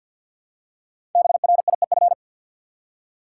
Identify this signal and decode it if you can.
Morse code, heard as a keyed tone, about 47 wpm, 700 Hz. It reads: BRIEF
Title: Five Letter Words - 50wpm